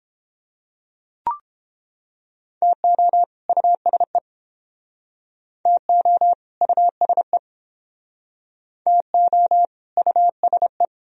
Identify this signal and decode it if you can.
Morse code, heard as a keyed tone, about 31 wpm, 700 Hz. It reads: TO USE TO USE TO USE